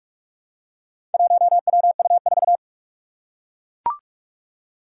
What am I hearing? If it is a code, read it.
Morse code, heard as a keyed tone, about 42 wpm, 700 Hz. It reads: 1WU4
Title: Four Challenging Letter-Number Combinations 45wpm